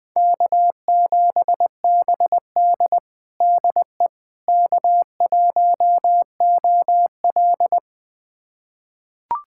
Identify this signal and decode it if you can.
Morse code, heard as a keyed tone, about 21 wpm, 700 Hz. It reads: K7BD DE K1OL
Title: USA Callsign DE Callsign - 4 Characters - 20wpm